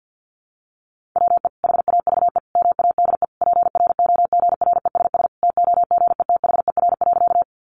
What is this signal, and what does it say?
Morse code, heard as a keyed tone, about 50 wpm, 700 Hz. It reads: WE HAVE MADE PROGRESS TOGETHER.
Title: Sentences from Top 1400 Words 50wpm